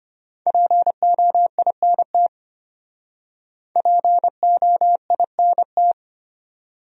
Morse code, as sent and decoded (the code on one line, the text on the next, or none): .--. --- .. -. - / .--. --- .. -. -
POINT POINT